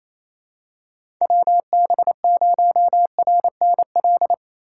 Morse code, heard as a keyed tone, about 28 wpm, 700 Hz. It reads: WB0RNL
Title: USA Call Signs - 3x Character Spacing - 28wpm